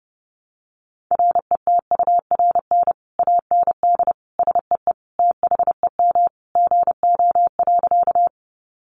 Morse code, heard as a keyed tone, about 28 wpm, 700 Hz. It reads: RETURN AND SEE THEM GO.
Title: Sentences from Top 800 Words 30wpm